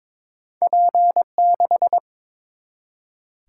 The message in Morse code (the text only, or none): P6